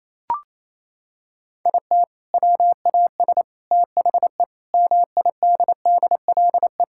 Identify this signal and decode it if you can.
Morse code, heard as a keyed tone, about 28 wpm, 700 Hz. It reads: IT WAS THE MIDDLE
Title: Sentences from Top 1000 Words - Review Part 6 - 28wpm